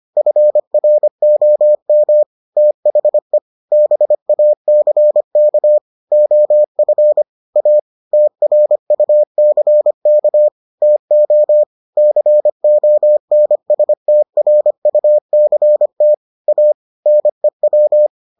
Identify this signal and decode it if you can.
Morse code, heard as a keyed tone, about 25 wpm, 600 Hz. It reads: FROM THE BACK OF A TRUCK TO CONSTRUCT A NEW